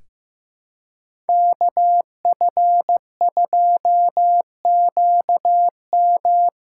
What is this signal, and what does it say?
Morse code, heard as a keyed tone, about 15 wpm, 700 Hz. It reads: KF2QM
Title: USA Call Signs - 5 Characters - 2x Character Spacing - 15wpm